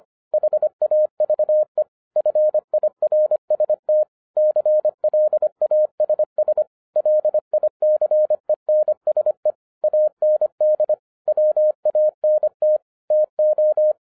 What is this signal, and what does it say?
Morse code, heard as a keyed tone, about 25 wpm, 610 Hz. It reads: HAVE FIRST CLASS LICENSE AND WANT TO